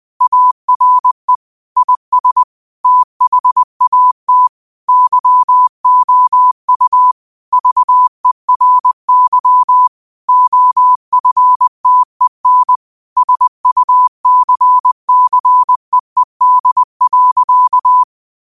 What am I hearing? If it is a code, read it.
Morse code, heard as a keyed tone, about 20 wpm, 1000 Hz. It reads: ARE IS THAT YOU VERY OFTEN SUCCEED.
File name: 20wpm